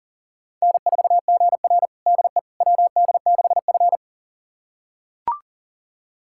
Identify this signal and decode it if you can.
Morse code, heard as a keyed tone, about 37 wpm, 700 Hz. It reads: N4GR DE WD6F
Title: USA Callsign DE Callsign - 4 Characters - 40wpm